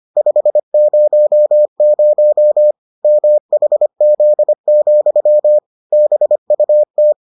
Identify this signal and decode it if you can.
Morse code, heard as a keyed tone, about 25 wpm, 600 Hz. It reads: 500 MHZ, BUT